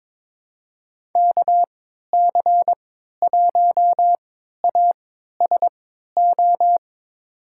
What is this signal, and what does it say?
Morse code, heard as a keyed tone, about 21 wpm, spaced as for 12 wpm, 700 Hz. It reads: KC1ASO
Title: USA Call Signs - 6 Characters - 3x Character Spacing - 22wpm